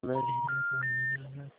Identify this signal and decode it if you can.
Special information tone, with speech.